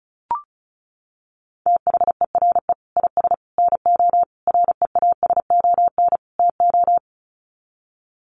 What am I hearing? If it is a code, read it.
Morse code, heard as a keyed tone, about 34 wpm, 700 Hz. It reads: THERE IS NO REASON TO
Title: Sets of 5 Words Encore 35wpm